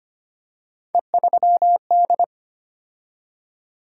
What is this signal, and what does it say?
Morse code, heard as a keyed tone, about 25 wpm, 700 Hz. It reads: E3D
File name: Three Letter-Number Combinations - 25wpm